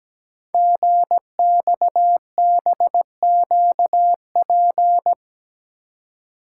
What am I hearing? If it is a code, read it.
Morse code, heard as a keyed tone, about 17 wpm, 700 Hz. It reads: GXBQP
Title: Code Groups - 5 Letters - 17wpm